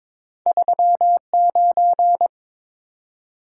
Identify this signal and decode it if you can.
Morse code, heard as a keyed tone, about 22 wpm, 700 Hz. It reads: 39
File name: practice / 22wpm / Two Letter-Number Combinations - 22wpm